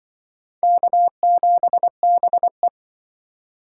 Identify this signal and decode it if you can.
Morse code, heard as a keyed tone, about 25 wpm, 700 Hz. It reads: K7BE
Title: Speed Racing - USA Call Signs 20wpm